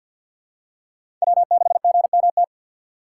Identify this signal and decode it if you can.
Morse code, heard as a keyed tone, about 48 wpm, 700 Hz. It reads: W6GMT
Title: USA Call Signs - 5 Characters - 6x Character Spacing - 50wpm